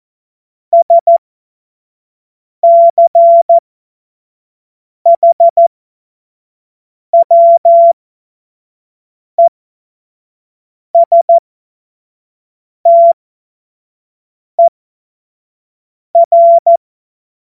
Morse code, heard as a keyed tone, about 14 wpm, spaced as for 5 wpm, 680 Hz. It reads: SCHWESTER
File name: Morsekurs / Witze / Witz 04. Juli 2023 25 F70